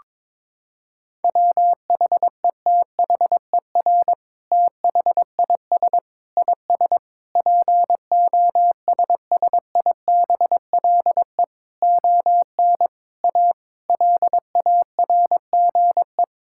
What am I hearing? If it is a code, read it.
Morse code, heard as a keyed tone, about 22 wpm, 700 Hz. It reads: WHETHER THIS IS POSSIBLE ON A LARGE
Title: Sentences from Top 1000 Words - Review Part 5 - 22wpm